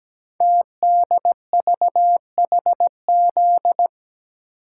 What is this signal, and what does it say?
Morse code, heard as a keyed tone, about 17 wpm, 700 Hz. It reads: TDVHZ
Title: Code Groups - 5 Letters - 17wpm